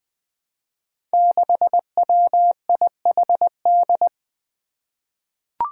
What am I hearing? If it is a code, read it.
Morse code, heard as a keyed tone, about 20 wpm, 700 Hz. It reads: 6WIHD
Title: Five Challenging Letter-Number Combinations 20wpm